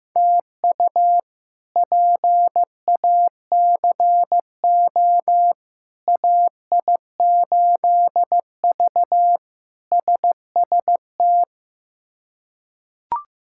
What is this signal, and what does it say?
Morse code, heard as a keyed tone, about 15 wpm, 700 Hz. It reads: TU PACO AI8V SST
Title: K1USN SST QSO Practice - 15wpm